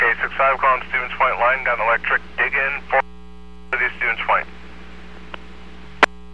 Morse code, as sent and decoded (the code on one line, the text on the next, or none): none